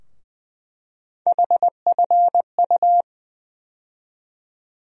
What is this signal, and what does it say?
Morse code, heard as a keyed tone, about 20 wpm, 700 Hz. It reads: HFU
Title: USA Call Sign Postfixes - 3 Letters 20wpm